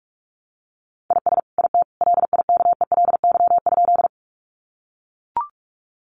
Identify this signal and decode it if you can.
Morse code, heard as a keyed tone, about 44 wpm, 700 Hz. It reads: IS IT LIKELY?
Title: Sentences from Top 600 Words 45wpm